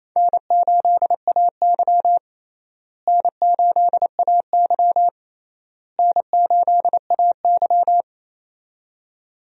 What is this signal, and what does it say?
Morse code, heard as a keyed tone, about 27 wpm, 700 Hz. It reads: N8AY N8AY N8AY
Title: USA Call Signs - 4 Characters - Repeated 3x - 28wpm